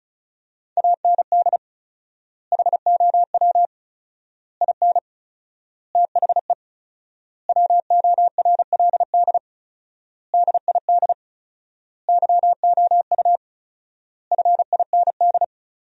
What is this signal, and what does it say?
Morse code, heard as a keyed tone, about 35 wpm, 700 Hz. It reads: AND HOW IN THE WORLD DID YOU FIND
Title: 4x Word-Spacing - Sentences from Top 200 Words - 35wpm